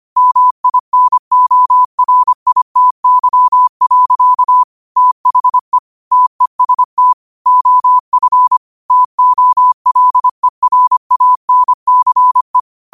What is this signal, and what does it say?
Morse code, heard as a keyed tone, about 25 wpm, 1000 Hz. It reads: MINORITY. THE TEST OF TOLERANCE
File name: qotdfiles / 25wpm